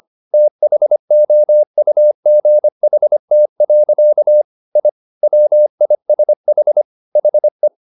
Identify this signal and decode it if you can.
Morse code, heard as a keyed tone, about 25 wpm, 600 Hz. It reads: THOUGHT. I WISH HE
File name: the old man and the sea part 1 25wpm